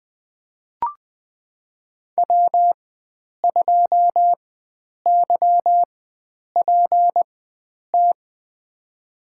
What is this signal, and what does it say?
Morse code, heard as a keyed tone, about 20 wpm, spaced as for 9 wpm, 700 Hz. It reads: W2YPT
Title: USA Call Signs - 5 Characters - 4x Character Spacing - 20wpm